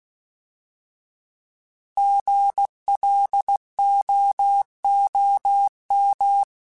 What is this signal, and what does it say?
Morse code, heard as a keyed tone, about 16 wpm, 800 Hz. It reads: GLOOM
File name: Alphabetical / Alphabetical mp3s / Gs @15